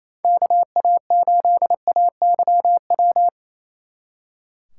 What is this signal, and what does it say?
Morse code, heard as a keyed tone, about 33 wpm, 700 Hz. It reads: KA8AYW
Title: Speed Racing - USA 2x3 Call Signs 28wpm